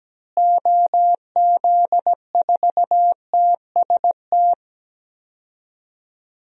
Morse code, heard as a keyed tone, about 17 wpm, 700 Hz. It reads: OZ4TST
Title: DXCC-100 Call Signs - 6 Characters - Repeated 3x - 17wpm